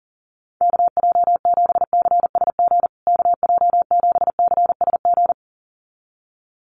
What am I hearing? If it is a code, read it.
Morse code, heard as a keyed tone, about 40 wpm, 700 Hz. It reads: KJ7CSG KJ7CSG
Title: USA Call Signs Repeated 2x 40wpm